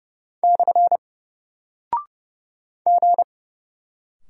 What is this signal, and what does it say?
Morse code, heard as a keyed tone, about 30 wpm, 700 Hz. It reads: / G
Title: Single Character (Letter:Number:Punctuation:Prosign) 30wpm